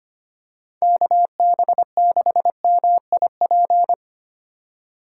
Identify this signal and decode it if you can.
Morse code, heard as a keyed tone, about 25 wpm, 700 Hz. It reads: KB6MIP
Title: USA Call Signs - 4x Character Spacing - 25wpm